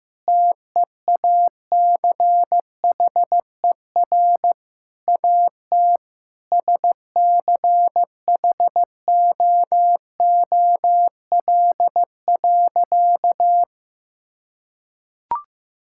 Morse code, heard as a keyed tone, about 15 wpm, 700 Hz. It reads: TEACHER AT SCHOOL.
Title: Sentences from Top 1000 Words - Review Part 6 - 15wpm